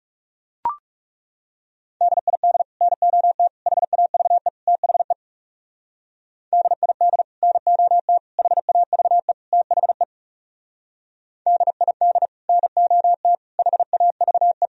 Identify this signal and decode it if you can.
Morse code, heard as a keyed tone, about 37 wpm, 700 Hz. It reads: DID NOT HAVE THE DID NOT HAVE THE DID NOT HAVE
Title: Speed-Racing - Sets of 4 Words 30wpm